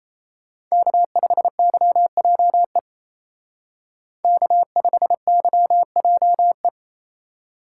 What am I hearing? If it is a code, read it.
Morse code, heard as a keyed tone, about 34 wpm, 700 Hz. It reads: K5YJE K5YJE
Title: Speed Racing - USA 1x3 Call Signs 28wpm